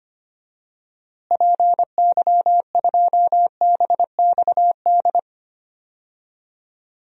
Morse code, heard as a keyed tone, about 25 wpm, 700 Hz. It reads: PY2BXD
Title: DXCC-100 Call Signs - 6 Characters - Repeated 2x - 25wpm